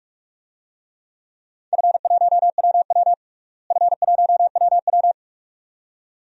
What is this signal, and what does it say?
Morse code, heard as a keyed tone, about 46 wpm, 700 Hz. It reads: F1WW F1WW
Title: DXCC-100 Call Signs - 4 Characters - Repeated 2x - 45wpm